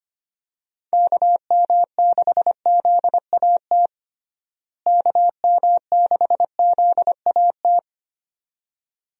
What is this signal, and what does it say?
Morse code, heard as a keyed tone, about 25 wpm, 700 Hz. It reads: KM6ZAT KM6ZAT
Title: USA Call Signs - 6 Characters - Repeated 2x - 25wpm